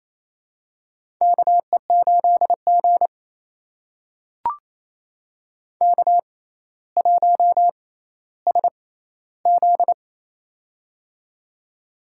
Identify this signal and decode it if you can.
Morse code, heard as a keyed tone, about 28 wpm, spaced as for 14 wpm, 700 Hz. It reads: KE8G K1SZ
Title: USA Call Signs - 4 Characters - 6x Character Spacing - 28wpm